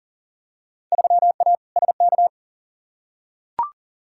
Morse code, heard as a keyed tone, about 40 wpm, 700 Hz. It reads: 3A SK
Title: Field Day Exchanges Encore - 40wpm